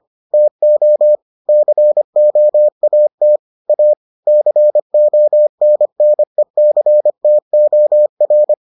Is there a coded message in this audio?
Yes: Morse code from 0.3 to 8.5 s, heard as a keyed tone, about 25 wpm, 600 Hz. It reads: TO COAT A CONNECTOR